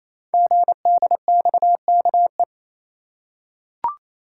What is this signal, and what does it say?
Morse code, heard as a keyed tone, about 27 wpm, 700 Hz. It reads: GDXKE
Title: Code Groups - 5 Letters - 28wpm